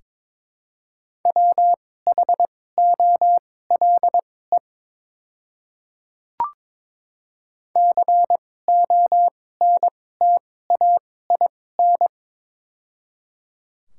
Morse code, heard as a keyed tone, about 22 wpm, spaced as for 16 wpm, 700 Hz. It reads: WHOLE CONTAIN
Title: 2x Character Spacing - Top 500 Words - 22wpm